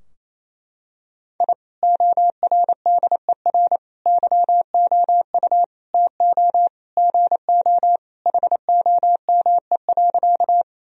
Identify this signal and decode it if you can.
Morse code, heard as a keyed tone, about 28 wpm, 700 Hz. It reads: I ORDER YOU TO GO HOME.
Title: Sentences from Top 1000 Words - Review Part 2 - 28wpm